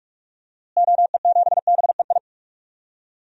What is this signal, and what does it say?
Morse code, heard as a keyed tone, about 44 wpm, 700 Hz. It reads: OE7BEI
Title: DXCC-100 Call Signs - 6 Characters - Repeated 2x - 45wpm